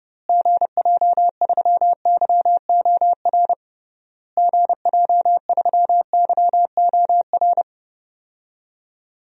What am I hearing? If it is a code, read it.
Morse code, heard as a keyed tone, about 30 wpm, 700 Hz. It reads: GJ3YOR GJ3YOR
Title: DXCC-100 Call Signs - 6 Characters - Repeated 3x - 30wpm